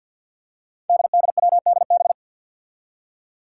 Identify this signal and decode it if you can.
Morse code, heard as a keyed tone, about 51 wpm, 700 Hz. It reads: DDWDB
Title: Five Challenging Letter-Number Combinations 50wpm